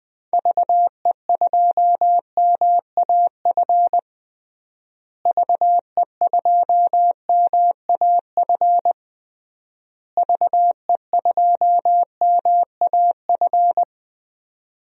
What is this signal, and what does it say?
Morse code, heard as a keyed tone, about 20 wpm, 700 Hz. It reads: VE2MAF VE2MAF VE2MAF
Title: DXCC-100 Call Signs - 6 Characters - Repeated 3x - 20wpm